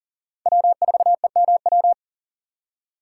Morse code, heard as a keyed tone, about 50 wpm, 700 Hz. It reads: W4EMW
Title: Speed Racing - USA 1x3 Call Signs 40wpm